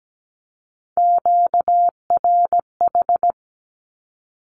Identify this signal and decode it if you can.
Morse code, heard as a keyed tone, about 17 wpm, 700 Hz. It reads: QRH
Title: Q-Codes with Spoken Meaning - 17wpm